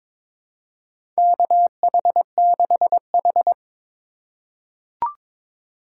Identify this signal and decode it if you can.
Morse code, heard as a keyed tone, about 22 wpm, 700 Hz. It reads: KH6H
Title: USA Call Signs - 3x Character Spacing - 22wpm